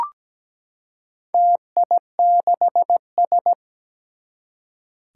Morse code, heard as a keyed tone, about 17 wpm, 700 Hz. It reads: TI6S
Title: Four Challenging Letter-Number Combinations 17wpm